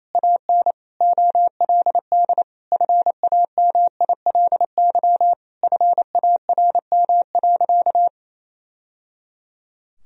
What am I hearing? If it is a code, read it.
Morse code, heard as a keyed tone, about 28 wpm, 700 Hz. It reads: AN OLD FAMILY FARM.
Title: Sentences from Top 1000 Words - Review Part 2 - 28wpm